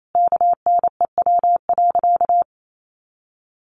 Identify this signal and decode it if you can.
Morse code, heard as a keyed tone, about 28 wpm, 700 Hz. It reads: KNEW.
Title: Sentences from Top 400 Words 28wpm